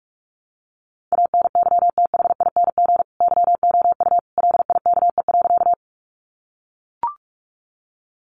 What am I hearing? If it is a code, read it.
Morse code, heard as a keyed tone, about 44 wpm, 700 Hz. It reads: ANYTHING YOU LIKE.
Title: Sentences from Top 1000 Words 45wpm